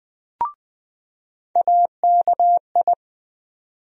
Morse code, heard as a keyed tone, about 21 wpm, 700 Hz. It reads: AKI
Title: USA Call Sign Postfixes - 3 Letters 20wpm